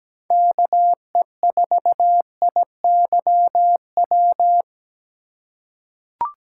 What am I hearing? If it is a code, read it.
Morse code, heard as a keyed tone, about 17 wpm, 700 Hz. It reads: KE4IYW
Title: USA Call Signs - 6 Characters - 4x Character Spacing - 17wpm